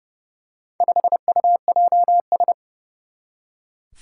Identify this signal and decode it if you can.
Morse code, heard as a keyed tone, about 32 wpm, 700 Hz. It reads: 5UJS
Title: Four Challenging Letter-Number Combinations 30wpm